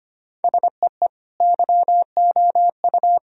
Morse code, heard as a keyed tone, about 25 wpm, 700 Hz. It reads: SEE YOU
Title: Sentences from Top 1000 Words - Review Part 4 - 25wpm